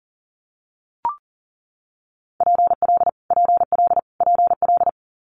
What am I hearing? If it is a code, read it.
Morse code, heard as a keyed tone, about 40 wpm, 690 Hz. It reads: PL PL PL
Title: Multi-letter Phonemes Repeated 3x 40wpm